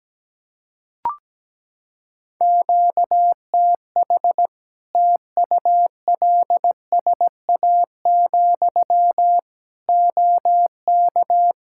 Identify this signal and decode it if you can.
Morse code, heard as a keyed tone, about 17 wpm, 700 Hz. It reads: QTH TULSA, OK
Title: Practice QSO - Exchange 1 - Beginning - 17wpm